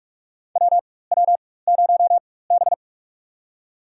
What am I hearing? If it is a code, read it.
Morse code, heard as a keyed tone, about 45 wpm, spaced as for 21 wpm, 700 Hz. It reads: WW0B
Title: USA Call Signs - 4 Characters - 4x Character Spacing - 45wpm